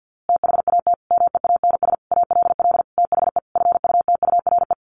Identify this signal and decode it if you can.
Morse code, heard as a keyed tone, about 50 wpm, 700 Hz. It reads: THAT MEANS ALL THE FUTURE
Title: A Sound Of Thunder 50wpm